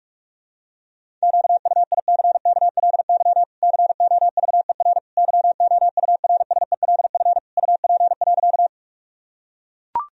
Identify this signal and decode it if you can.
Morse code, heard as a keyed tone, about 44 wpm, 700 Hz. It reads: QUICKLY COVER YOURSELF UP.
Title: Sentences from Top 1000 Words - Review Part 1 - 45wpm